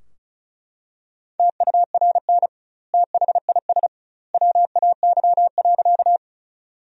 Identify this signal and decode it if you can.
Morse code, heard as a keyed tone, about 34 wpm, 700 Hz. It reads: TURN THIS WAY.
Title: 2x Word-Spacing - Sentences from Top 200 Words - 35wpm